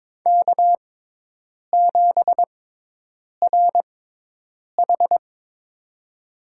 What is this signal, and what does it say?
Morse code, heard as a keyed tone, about 22 wpm, spaced as for 8 wpm, 700 Hz. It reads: K7RH
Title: USA Call Signs - 4 Characters - 6x Character Spacing - 22wpm